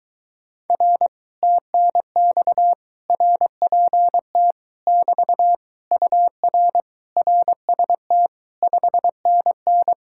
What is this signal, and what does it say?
Morse code, heard as a keyed tone, about 23 wpm, 700 Hz. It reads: R TNX RPT <BT> UR RST 5NN